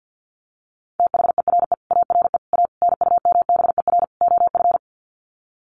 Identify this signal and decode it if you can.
Morse code, heard as a keyed tone, about 50 wpm, 700 Hz. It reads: THERE ARE A NUMBER OF
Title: Sets of 5 Words 50wpm